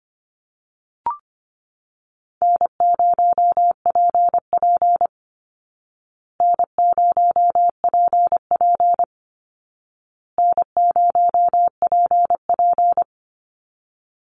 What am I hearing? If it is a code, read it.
Morse code, heard as a keyed tone, about 25 wpm, 700 Hz. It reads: N0PP N0PP N0PP
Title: Easy Call Signs Repeated 3x - 25wpm